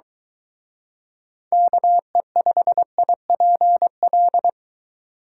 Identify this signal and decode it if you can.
Morse code, heard as a keyed tone, about 22 wpm, 700 Hz. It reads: KE5IPL